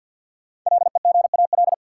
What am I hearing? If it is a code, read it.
Morse code, heard as a keyed tone, about 49 wpm, 700 Hz. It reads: LEGAL
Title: Sentences from Top 1200 Words - New Word Drill - 50wpm